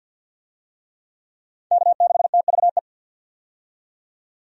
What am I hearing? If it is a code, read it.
Morse code, heard as a keyed tone, about 45 wpm, 700 Hz. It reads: K6TVE
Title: USA Call Signs - 5 Characters - 50wpm